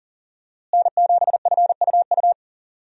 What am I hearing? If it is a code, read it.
Morse code, heard as a keyed tone, about 40 wpm, 700 Hz. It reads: N7FUU
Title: USA Call Signs - 5 Characters - Repeated 3x - 40wpm